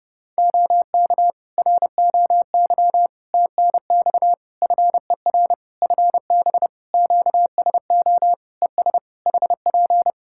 Morse code, heard as a keyed tone, about 30 wpm, 700 Hz. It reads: OK ROY TNX FER FB QSO ES HP
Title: Practice QSO - Exchange 4 - Ending - 30wpm